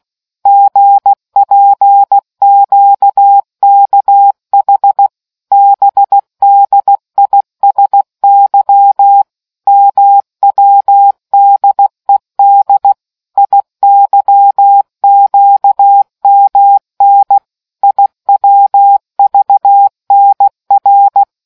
Morse code, heard as a keyed tone, about 16 wpm, 800 Hz. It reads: GPQKH BDISY MWDED IYQMN IWVNR